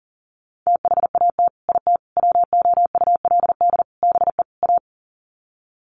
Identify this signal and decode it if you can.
Morse code, heard as a keyed tone, about 40 wpm, 700 Hz. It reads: THAT IT WOULD BE A
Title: Sets of 5 Words Encore2 40wpm